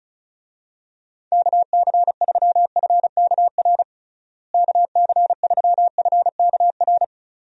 Morse code, heard as a keyed tone, about 36 wpm, 700 Hz. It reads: KC3FKR KC3FKR
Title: USA Call Signs - 6 Characters - Repeated 2x - 35wpm